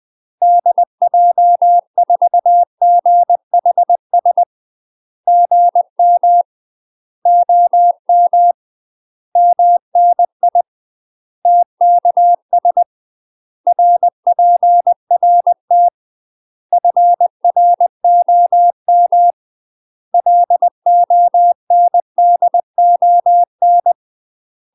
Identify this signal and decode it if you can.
Morse code, heard as a keyed tone, about 20 wpm, 700 Hz. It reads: DJ4GHS GM OM MNI TKS RPRT FROM LONDON
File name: UK MASTER QSOs TRAINING / 20WPM / UK MASTER QSOs 0055